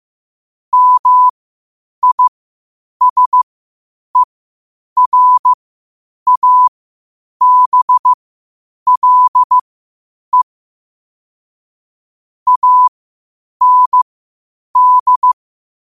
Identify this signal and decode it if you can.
Morse code, heard as a keyed tone, about 15 wpm, spaced as for 8 wpm, 1000 Hz. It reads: MISERABLE AND
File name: qotdfiles / 8wpm